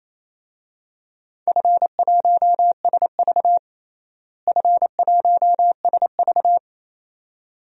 Morse code, heard as a keyed tone, about 28 wpm, 700 Hz. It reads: F1SV F1SV
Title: DXCC-100 Call Signs - 4 Characters - Repeated 2x - 28wpm